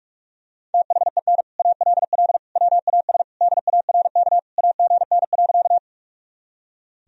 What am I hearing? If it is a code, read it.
Morse code, heard as a keyed tone, about 44 wpm, 700 Hz. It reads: THEN ALL WAS DARK AGN.
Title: Sentences from Top 500 Words and Abbreviations 45wpm